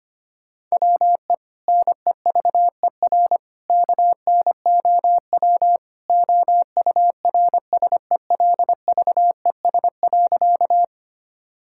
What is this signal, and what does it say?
Morse code, heard as a keyed tone, about 25 wpm, 700 Hz. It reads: WE NEVER KNOW OURSELVES.
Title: Sentences from Top 1600 Words 25wpm